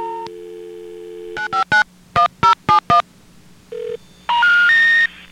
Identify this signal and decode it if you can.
DTMF, heard as telephone touch tones, tones of about 120 ms. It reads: 9591071